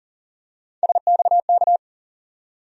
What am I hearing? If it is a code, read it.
Morse code, heard as a keyed tone, about 40 wpm, 700 Hz. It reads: SXK
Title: USA Call Sign Postfixes - 3 Letters 40wpm